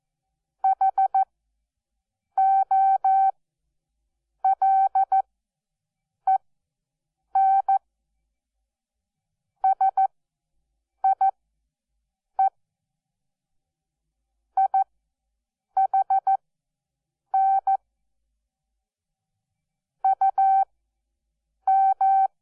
Morse code, heard as a keyed tone, about 14 wpm, spaced as for 7 wpm, 770 Hz. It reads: HOLEN SIE IHN UM
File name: Onlinelehrgang / morsen / DJ4UF Morsen12